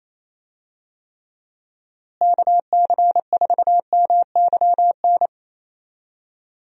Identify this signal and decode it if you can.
Morse code, heard as a keyed tone, about 28 wpm, 700 Hz. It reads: KC4MYN
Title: USA Call Signs - 6 Characters - Repeated 2x - 28wpm